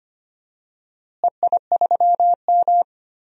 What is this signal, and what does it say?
Morse code, heard as a keyed tone, about 25 wpm, 700 Hz. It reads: EI3M
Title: DXCC-100 Call Signs - 4 Characters - Repeated 3x - 25wpm